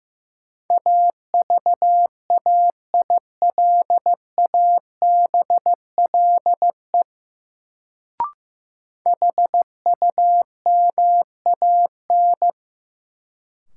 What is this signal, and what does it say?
Morse code, heard as a keyed tone, about 15 wpm, 700 Hz. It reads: AVAILABLE HUMAN
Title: Sentences from Top 600 Words - New Word Drill - 15wpm